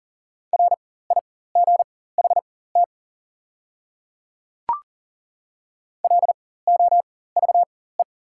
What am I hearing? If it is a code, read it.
Morse code, heard as a keyed tone, about 40 wpm, spaced as for 19 wpm, 700 Hz. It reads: RIGHT LOVE
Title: 4x Character Spacing - Top 500 Words - 40wpm